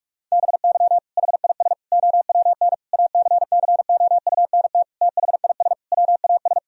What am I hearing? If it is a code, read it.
Morse code, heard as a keyed tone, about 41 wpm, 700 Hz. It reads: BY HIS OWN ACCOUNT THIS WAS
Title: Sentences from Top 1000 Words - Review Part 1 - 45wpm